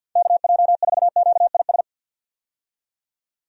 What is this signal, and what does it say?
Morse code, heard as a keyed tone, about 51 wpm, 700 Hz. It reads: KJ4QIS
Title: USA Call Signs - 6 Characters - 50wpm